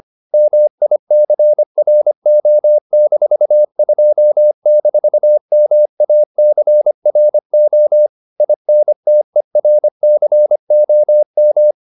Morse code, heard as a keyed tone, about 25 wpm, 600 Hz. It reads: MICRO-2-MACRO INTERCOM